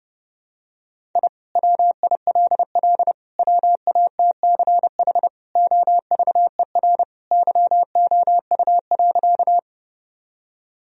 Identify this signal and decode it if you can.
Morse code, heard as a keyed tone, about 32 wpm, 700 Hz. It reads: I WILL WATCH OVER YOU.
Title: Sentences from Top 1000 Words - Review Part 1 - 30wpm